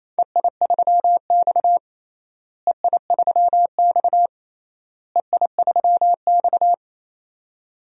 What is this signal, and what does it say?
Morse code, heard as a keyed tone, about 27 wpm, 700 Hz. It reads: EI3X EI3X EI3X
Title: DXCC-100 Call Signs - 4 Characters - Repeated 3x - 28wpm